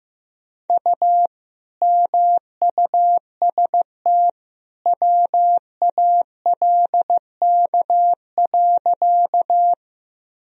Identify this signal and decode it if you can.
Morse code, heard as a keyed tone, about 15 wpm, 700 Hz. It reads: U MUST WALK.
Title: Sentences from Top 300 Words and Abbreviations 15wpm